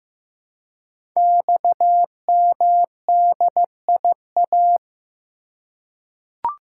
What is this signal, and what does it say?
Morse code, heard as a keyed tone, about 15 wpm, 700 Hz. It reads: XMDIA
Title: Code Groups - 5 Letters - 15wpm